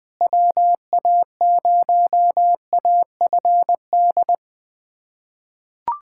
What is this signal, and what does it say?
Morse code, heard as a keyed tone, about 20 wpm, 700 Hz. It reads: WA0AFD
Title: USA Callsign DE Callsign 20wpm